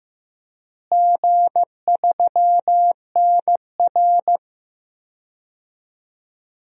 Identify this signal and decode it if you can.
Morse code, heard as a keyed tone, about 15 wpm, 700 Hz. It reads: G3NR
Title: DXCC-100 Call Signs - 4 Characters - Repeated 2x - 15wpm